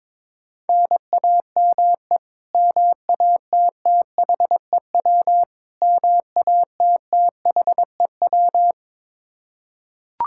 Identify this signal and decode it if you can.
Morse code, heard as a keyed tone, about 21 wpm, 700 Hz. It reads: NAME MATTHEW MATTHEW
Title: K1USN SST QSO Practice 22wpm